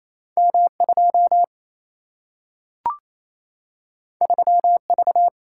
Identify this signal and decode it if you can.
Morse code, heard as a keyed tone, about 28 wpm, 700 Hz. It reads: M 2 3 V
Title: Single-Letter Single-Number 28wpm